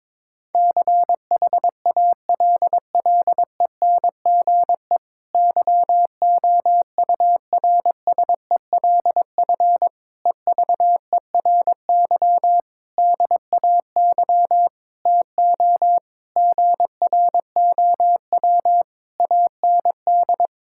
Morse code, heard as a keyed tone, about 22 wpm, 700 Hz. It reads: CHALLENGE YOURSELF EVERY DAY TO GROW AND
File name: Difficult Sentences with Hints - 22wpm